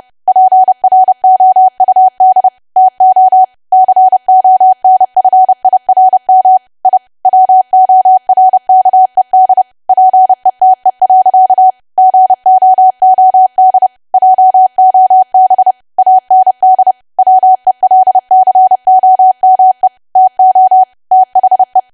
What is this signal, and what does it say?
Morse code, heard as a keyed tone, about 30 wpm, 750 Hz. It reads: PROUD TO CONFIRM I WORKED PETE. GOOD JOB AND WELCOME TO THE
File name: cwa / W1AW-Level3 / 30WPM